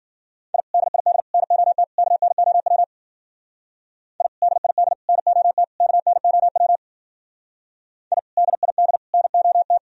Morse code, heard as a keyed tone, about 48 wpm, 700 Hz. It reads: I DID NOT KNOW I DID NOT KNOW I DID NOT
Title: Speed-Racing - Sets of 4 Words 40wpm